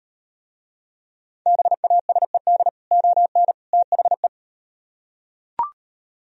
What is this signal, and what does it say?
Morse code, heard as a keyed tone, about 36 wpm, 700 Hz. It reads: BASED ON THE